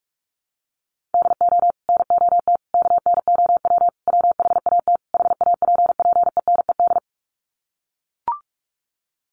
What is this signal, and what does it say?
Morse code, heard as a keyed tone, about 44 wpm, 690 Hz. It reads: DO NOT KNOW WHAT HAPPENED
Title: Sets of 5 Words Encore2 45wpm